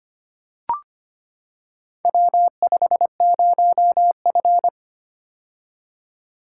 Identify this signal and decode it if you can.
Morse code, heard as a keyed tone, about 25 wpm, 700 Hz. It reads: W50F
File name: practice / 25wpm / Four Letter-Number Combinations - 25wpm